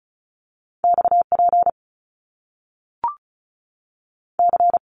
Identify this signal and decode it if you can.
Morse code, heard as a keyed tone, about 36 wpm, 700 Hz. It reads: X P C
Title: Top Two Letter Combinations in Words 35wpm